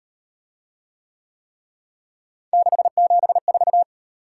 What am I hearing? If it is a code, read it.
Morse code, heard as a keyed tone, about 36 wpm, 700 Hz. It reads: 674